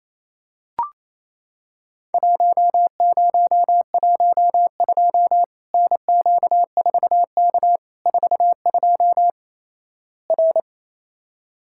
Morse code, heard as a keyed tone, about 28 wpm, 700 Hz. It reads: 1012 NQ4K 42 R
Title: Worked All Europe DX Contest - QTC Exchange - 28wpm